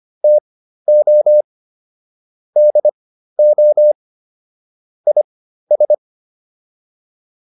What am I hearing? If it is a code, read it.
Morse code, heard as a keyed tone, about 25 wpm, spaced as for 13 wpm, 600 Hz. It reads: TO DO IS